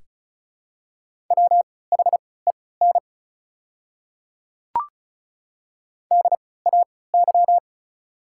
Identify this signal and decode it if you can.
Morse code, heard as a keyed tone, about 34 wpm, spaced as for 20 wpm, 700 Hz. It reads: WHEN DAY
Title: 3x Character Spacing - Top 200 Words - 35wpm